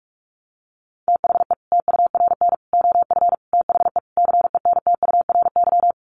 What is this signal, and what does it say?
Morse code, heard as a keyed tone, about 44 wpm, 700 Hz. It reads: THE TURN OF THE CENTURY
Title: Sets of 5 Words 45wpm